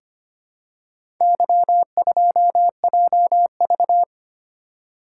Morse code, heard as a keyed tone, about 25 wpm, 700 Hz. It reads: Y2JV
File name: practice / 25wpm / Four Letter-Number Combinations - 25wpm